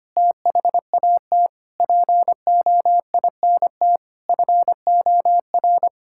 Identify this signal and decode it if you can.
Morse code, heard as a keyed tone, about 25 wpm, 700 Hz. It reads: THAT POINT FOR
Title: Sentences from Top 1000 Words - Review Part 6 - 25wpm